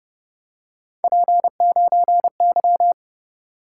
Morse code, heard as a keyed tone, about 29 wpm, 700 Hz. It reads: P9Y